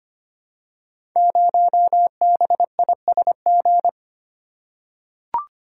Morse code, heard as a keyed tone, about 25 wpm, 700 Hz. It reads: 0BISG
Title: Code Groups - 5 Letter:Numbers - 25wpm